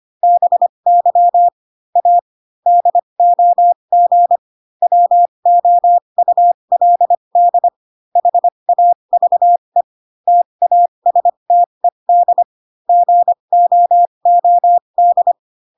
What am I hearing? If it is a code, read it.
Morse code, heard as a keyed tone, about 25 wpm, spaced as for 22 wpm, 700 Hz. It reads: BY A DOG WOULD HAVE TASTED GOOD